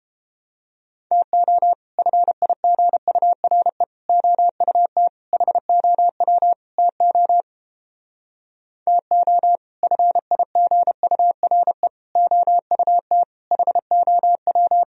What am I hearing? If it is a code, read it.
Morse code, heard as a keyed tone, about 29 wpm, 700 Hz. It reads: TO FIGURE OUT HOW TO TO FIGURE OUT HOW